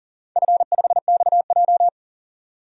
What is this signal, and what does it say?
Morse code, heard as a keyed tone, about 40 wpm, 700 Hz. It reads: F5XJ
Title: DXCC-100 Call Signs - 4 Characters - 40wpm